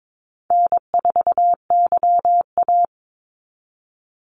Morse code, heard as a keyed tone, about 22 wpm, 700 Hz. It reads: N4YA
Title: Call Signs Repeated 3x - 22wpm